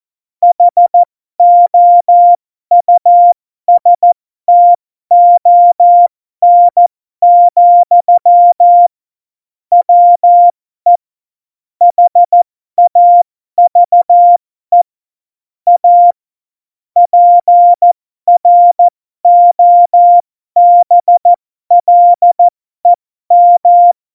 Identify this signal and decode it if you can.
Morse code, heard as a keyed tone, about 14 wpm, spaced as for 12 wpm, 700 Hz. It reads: HOUSTON, WE HAVE A PROBLEM
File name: Morsekurs / Witze